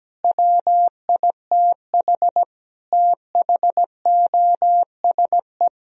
Morse code, heard as a keyed tone, about 17 wpm, 700 Hz. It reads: WITH THOSE
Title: Sentences from Top 1000 Words - Review Part 1 - 17wpm